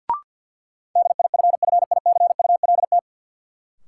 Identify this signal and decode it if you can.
Morse code, heard as a keyed tone, about 49 wpm, 700 Hz. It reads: DIFFICULT
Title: Sentences from Top 600 Words - New Word Drill - 50wpm